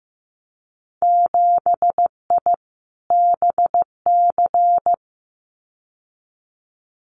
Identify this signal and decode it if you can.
Morse code, heard as a keyed tone, about 15 wpm, 700 Hz. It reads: 7I BC
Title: Winter Field Day Exchanges Encore 15wpm